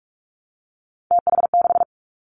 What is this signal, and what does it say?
Morse code, heard as a keyed tone, about 46 wpm, 700 Hz. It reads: TH6
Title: Three Challenging Letter-Number Combinations 45wpm